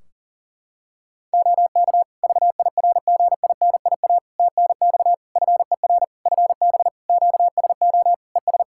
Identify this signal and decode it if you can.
Morse code, heard as a keyed tone, about 40 wpm, 700 Hz. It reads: OK VIRGINIA TNX FER FB QSO ES
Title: Practice QSO - Exchange 4 - Ending - 40wpm